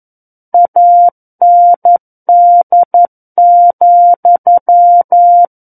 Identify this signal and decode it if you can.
Morse code, heard as a keyed tone, about 11 wpm, 700 Hz. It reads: AND,
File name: 11 WPM